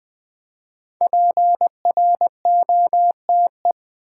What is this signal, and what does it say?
Morse code, heard as a keyed tone, about 20 wpm, 700 Hz. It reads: PROTE
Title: Build A Word - 5 Letters At A Time - 10-Letter Words - 20wpm